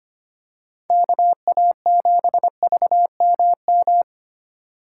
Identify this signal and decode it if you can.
Morse code, heard as a keyed tone, about 25 wpm, 700 Hz. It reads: KA7VMM
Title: USA Call Signs - 6 Characters - 2x Character Spacing - 25wpm